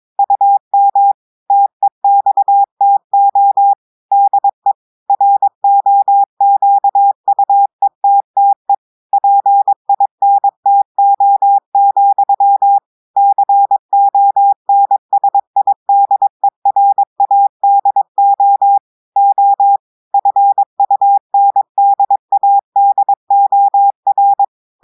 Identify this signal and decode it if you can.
Morse code, heard as a keyed tone, about 22 wpm, 800 Hz. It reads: UM TEXTO DE ROQUETTE PINTO, CONSIDERADO O FUNDADOR
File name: audios / portugais / portugais texte en clair 1 vitesse 22 mots minutes